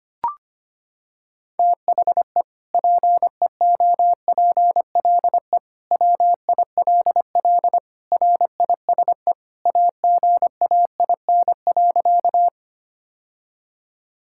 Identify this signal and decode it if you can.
Morse code, heard as a keyed tone, about 25 wpm, 700 Hz. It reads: THE PEOPLE WILL RISE AGAIN.
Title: Sentences from Top 1000 Words - Review Part 6 - 25wpm